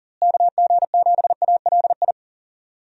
Morse code, heard as a keyed tone, about 40 wpm, 700 Hz. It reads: KG7ALI
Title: USA Call Signs - 6 Characters - Repeated 2x - 40wpm